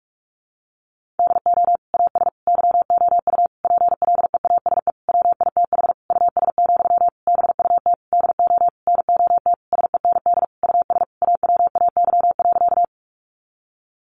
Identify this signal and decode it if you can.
Morse code, heard as a keyed tone, about 44 wpm, 700 Hz. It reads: DO AS YOU PLEASE WITH US, BUT DO NOT SEND US AWAY.
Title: Sentences from Top 1500 Words 45wpm